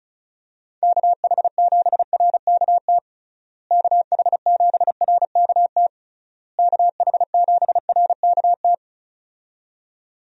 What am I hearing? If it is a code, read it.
Morse code, heard as a keyed tone, about 36 wpm, 700 Hz. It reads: KH7RKT KH7RKT KH7RKT
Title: USA Call Signs - 6 Characters - Repeated 3x - 35wpm